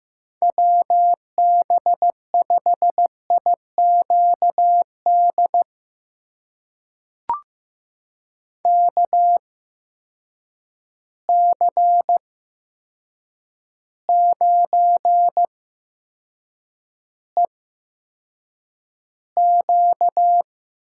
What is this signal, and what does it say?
Morse code, heard as a keyed tone, about 15 wpm, spaced as for 7 wpm, 700 Hz. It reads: WB5IQD KC9EQ
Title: USA Call Signs - 6 Characters - 8x Character Spacing - 15wpm